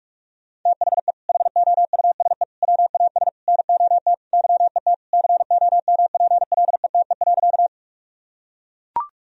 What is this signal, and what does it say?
Morse code, heard as a keyed tone, about 42 wpm, 700 Hz. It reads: THE HOUSE WAS NOT YET COMPLETE.
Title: Sentences from Top 1000 Words - Review Part 2 - 45wpm